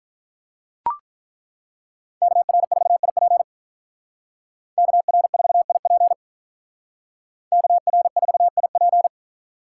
Morse code, heard as a keyed tone, about 43 wpm, 700 Hz. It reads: KR4IP KR4IP KR4IP
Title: Speed Racing - USA 2x2 Call Signs 35wpm